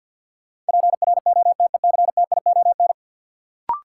Morse code, heard as a keyed tone, about 51 wpm, 700 Hz. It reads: PROTECTION
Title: Sentences from Top 1600 Words - New Word Drill Alt - 50wpm